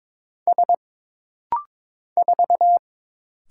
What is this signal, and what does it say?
Morse code, heard as a keyed tone, about 22 wpm, 700 Hz. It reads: S 4
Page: This rapid-fire timing is geared to push your ICR (Instant Character Recognition) skills.